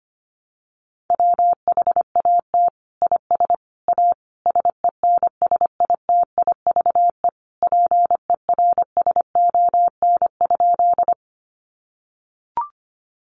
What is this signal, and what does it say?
Morse code, heard as a keyed tone, about 25 wpm, 700 Hz. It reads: WHAT IS A SENSITIVE PERSON?
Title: Sentences from Top 1400 Words 25wpm